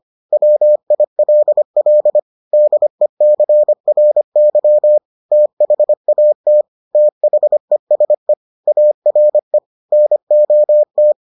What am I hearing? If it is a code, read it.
Morse code, heard as a keyed tone, about 25 wpm, 600 Hz. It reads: WILL DECRY THAT THESE ARE NOT